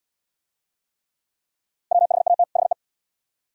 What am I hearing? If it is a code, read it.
Morse code, heard as a keyed tone, about 75 wpm, 690 Hz. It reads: WHAT HE